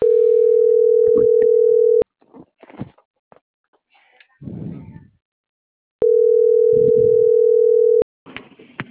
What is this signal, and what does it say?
Ringback tone (North America), 440 + 480 Hz.